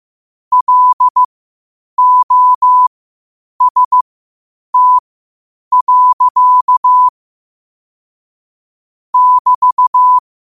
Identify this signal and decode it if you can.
Morse code, heard as a keyed tone, about 15 wpm, spaced as for 8 wpm, 1000 Hz. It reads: LOST. <BT>
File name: qotdfiles / 8wpm